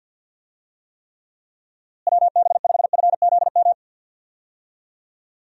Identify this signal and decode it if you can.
Morse code, heard as a keyed tone, about 52 wpm, 700 Hz. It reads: WB5FZM
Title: USA Call Signs - 6 Characters - Repeated 3x - 50wpm